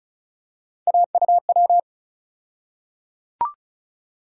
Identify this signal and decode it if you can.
Morse code, heard as a keyed tone, about 36 wpm, 700 Hz. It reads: AUW
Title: USA Call Sign Postfixes - 3 Letters 35wpm